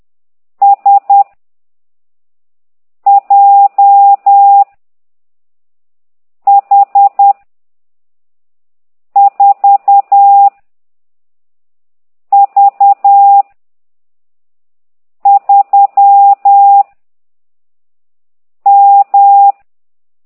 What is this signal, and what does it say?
Morse code, heard as a keyed tone, about 10 wpm, 800 Hz. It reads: S J H 4 V 3 M
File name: ct2idw / exerc15